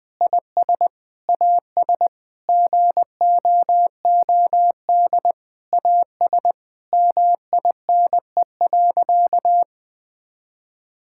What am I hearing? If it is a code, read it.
Morse code, heard as a keyed tone, about 20 wpm, 700 Hz. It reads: IS AS GOOD AS MINE.
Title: Sentences with 7-Word Phrases - Spoken Beginning - 20wpm